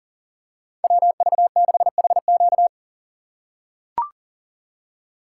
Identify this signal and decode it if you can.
Morse code, heard as a keyed tone, about 40 wpm, 700 Hz. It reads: WV6HQ
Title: USA Call Signs - 5 Characters - 6x Character Spacing - 40wpm